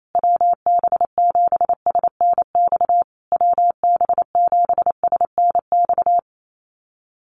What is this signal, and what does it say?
Morse code, heard as a keyed tone, about 28 wpm, 700 Hz. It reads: WB7SNX WB7SNX
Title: USA Call Signs Repeated 3x 28wpm